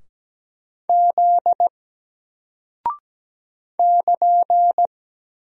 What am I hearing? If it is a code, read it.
Morse code, heard as a keyed tone, about 17 wpm, 700 Hz. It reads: Z <KN>
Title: Single Character (Letter:Number:Punctuation:Prosign) 17wpm